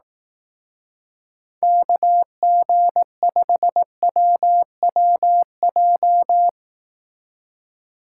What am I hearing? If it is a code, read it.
Morse code, heard as a keyed tone, about 18 wpm, 700 Hz. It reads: KG5WWJ
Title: Speed Racing - USA 2x3 Call Signs 15wpm